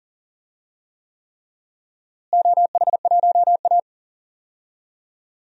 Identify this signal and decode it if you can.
Morse code, heard as a keyed tone, about 40 wpm, 700 Hz. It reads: OH1A
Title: DXCC-100 Call Signs - 4 Characters - Repeated 3x - 40wpm